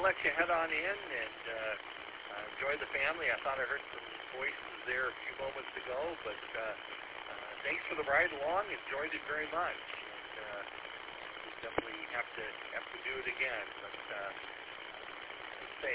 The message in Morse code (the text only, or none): none